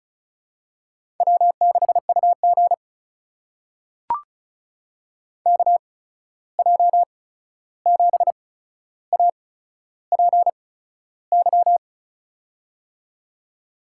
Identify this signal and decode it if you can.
Morse code, heard as a keyed tone, about 36 wpm, spaced as for 13 wpm, 700 Hz. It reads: W6UG KJ7APY
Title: USA Call Signs - 8x Character Spacing - 35wpm